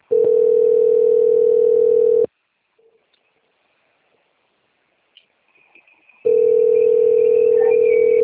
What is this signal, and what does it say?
Ringback tone (North America), 440 + 480 Hz.
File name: out- - - - - .73833